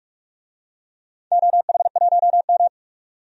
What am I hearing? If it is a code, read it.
Morse code, heard as a keyed tone, about 46 wpm, 700 Hz. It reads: OH1M
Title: DXCC-100 Call Signs - 4 Characters - 45wpm